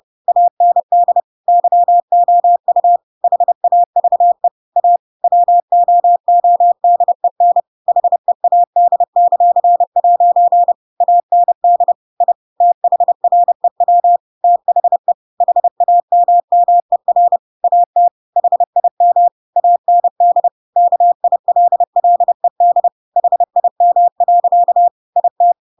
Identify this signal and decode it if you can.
Morse code, heard as a keyed tone, about 30 wpm, 700 Hz. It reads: AND YOU HAVE A WOODEN HEAD;' AND I THREW THE HAMMER AT HIM AND KILLED HIM. IT
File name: Pinocchio / Pinocchio ch7 30 30 ik1hge